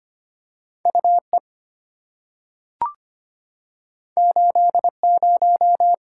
Morse code, heard as a keyed tone, about 25 wpm, 700 Hz. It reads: UE 80
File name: Two Letter-Number Combinations - 25wpm